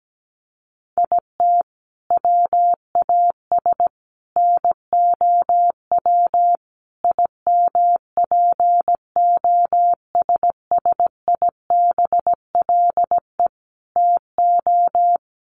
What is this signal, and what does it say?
Morse code, heard as a keyed tone, about 17 wpm, 700 Hz. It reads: IT WAS NOW IMPOSSIBLE TO
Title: Sentences from Top 1100 Words 17wpm